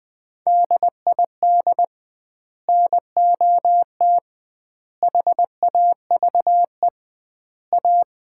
Morse code, heard as a keyed tone, about 20 wpm, 700 Hz. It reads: DID NOT HAVE A
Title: 2x Word-Spacing - Sets of 4 Words - 20wpm